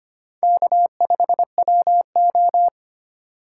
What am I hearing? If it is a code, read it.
Morse code, heard as a keyed tone, about 25 wpm, 700 Hz. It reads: K5WO
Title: USA Call Signs - 4 Characters - 4x Character Spacing - 25wpm